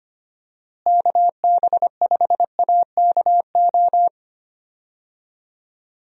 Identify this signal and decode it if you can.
Morse code, heard as a keyed tone, about 25 wpm, 700 Hz. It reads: KB5AKO
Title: USA Call Signs - 6 Characters - Repeated 2x - 25wpm